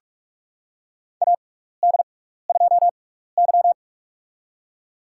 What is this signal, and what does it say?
Morse code, heard as a keyed tone, about 47 wpm, spaced as for 16 wpm, 700 Hz. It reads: AD2Y
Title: USA Call Signs - 6x Character Spacing - 45wpm